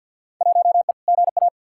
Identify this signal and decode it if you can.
Morse code, heard as a keyed tone, about 53 wpm, 700 Hz. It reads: 1E GA
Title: Field Day Exchanges - 50wpm